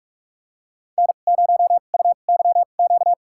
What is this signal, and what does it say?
Morse code, heard as a keyed tone, about 42 wpm, spaced as for 31 wpm, 690 Hz. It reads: N0UYQ
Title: USA Call Signs - 5 Characters - 2x Character Spacing - 45wpm